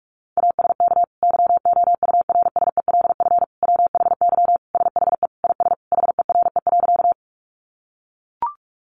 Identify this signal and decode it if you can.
Morse code, heard as a keyed tone, about 44 wpm, 700 Hz. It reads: ASK YOURSELF WHY SHE IS HERE.
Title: Sentences from Top 600 Words 45wpm